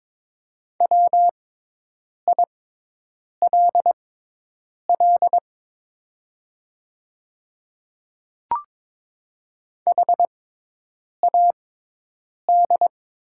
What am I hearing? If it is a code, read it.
Morse code, heard as a keyed tone, about 22 wpm, spaced as for 8 wpm, 700 Hz. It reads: WILL HAD
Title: 6x Character Spacing - Top 100 Words - 22wpm